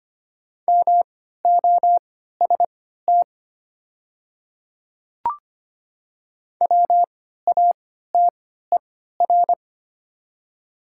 Morse code, heard as a keyed tone, about 25 wpm, spaced as for 14 wpm, 700 Hz. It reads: MOST WATER
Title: 3x Character Spacing - Top 500 Words - 25wpm